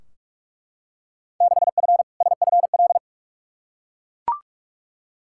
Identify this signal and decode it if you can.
Morse code, heard as a keyed tone, about 42 wpm, 700 Hz. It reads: 6F SFL
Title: Field Day Exchanges Encore - 45wpm